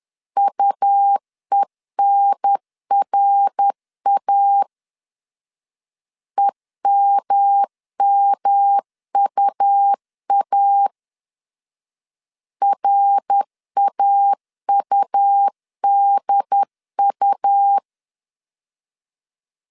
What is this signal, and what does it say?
Morse code, heard as a keyed tone, about 10 wpm, 800 Hz. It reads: UENRA EMMUA RAUDU
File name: CW / DET / 1 3 1500C S C 10wpm 3335